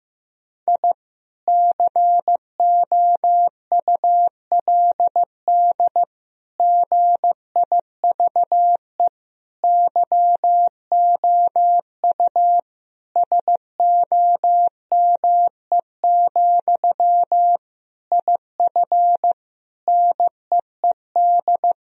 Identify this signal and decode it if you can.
Morse code, heard as a keyed tone, about 15 wpm, 700 Hz. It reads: I COULD GIVE YOU SOME, IF NEED
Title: Sentences from Top 1000 Words - Review Part 1 - 15wpm